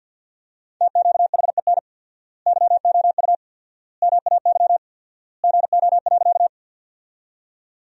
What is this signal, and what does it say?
Morse code, heard as a keyed tone, about 50 wpm, 700 Hz. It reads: TQHEN YOU MAY GO.
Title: 4x Word-Spacing - Sentences from Top 100 Words - 50wpm